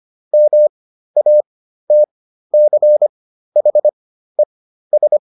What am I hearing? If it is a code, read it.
Morse code, heard as a keyed tone, about 25 wpm, spaced as for 13 wpm, 600 Hz. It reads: MATCHES